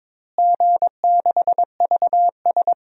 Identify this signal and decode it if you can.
Morse code, heard as a keyed tone, about 22 wpm, 700 Hz. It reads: G6VS
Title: DXCC-100 Call Signs - 4 Characters - Repeated 2x - 22wpm